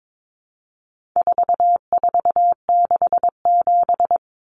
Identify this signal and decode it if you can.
Morse code, heard as a keyed tone, about 22 wpm, 700 Hz. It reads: 4467
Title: Numbers - 4-Digit 22wpm